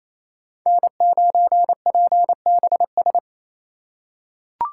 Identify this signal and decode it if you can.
Morse code, heard as a keyed tone, about 27 wpm, 690 Hz. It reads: N9PBS
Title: USA Call Signs - 3x Character Spacing - 28wpm